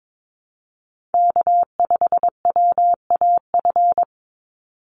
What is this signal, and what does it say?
Morse code, heard as a keyed tone, about 22 wpm, 700 Hz. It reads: K5WAF
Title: Call Signs Repeated 2x - 22wpm